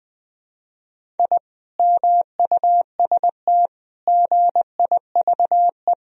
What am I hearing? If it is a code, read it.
Morse code, heard as a keyed tone, about 20 wpm, 700 Hz. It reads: I MUST GIVE
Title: Sentences from Top 1000 Words - Review Part 2 - 20wpm